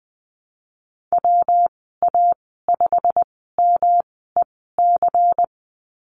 Morse code, heard as a keyed tone, about 20 wpm, spaced as for 15 wpm, 700 Hz. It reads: WA5MEC
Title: 122 Lesson 20g - Number 3 - Call Signs 20wpm